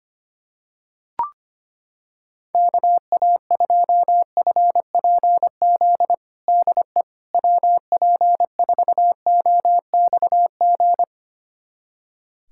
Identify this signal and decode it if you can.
Morse code, heard as a keyed tone, about 25 wpm, 700 Hz. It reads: KA2FPZ DE WP4OXG
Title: USA Callsign DE Callsign - 6 Characters - 25wpm